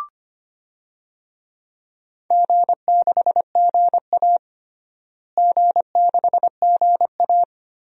Morse code, heard as a keyed tone, about 25 wpm, 700 Hz. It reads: G6GA G6GA
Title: DXCC-100 Call Signs - 4 Characters - Repeated 3x - 25wpm